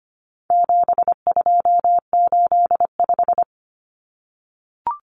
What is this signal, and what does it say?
Morse code, heard as a keyed tone, about 25 wpm, 700 Hz. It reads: 7285
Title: Common Cut Numbers - 4-Digit 25wpm